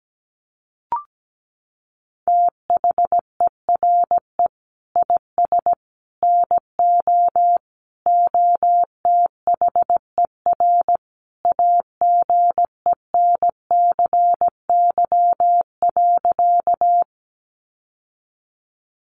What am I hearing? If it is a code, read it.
Morse code, heard as a keyed tone, about 17 wpm, 700 Hz. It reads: THERE IS NO OTHER AGENCY.
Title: Sentences from Top 1300 Words 17wpm